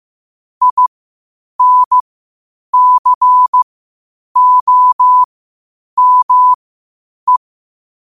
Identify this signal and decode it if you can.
Morse code, heard as a keyed tone, about 15 wpm, spaced as for 8 wpm, 1000 Hz. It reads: INCOME
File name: qotdfiles / 8wpm